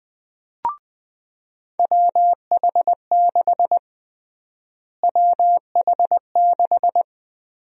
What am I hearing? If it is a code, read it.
Morse code, heard as a keyed tone, about 21 wpm, 700 Hz. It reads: WH6 WH6
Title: USA Call Sign Prefixes - Repeated 2x - 20wpm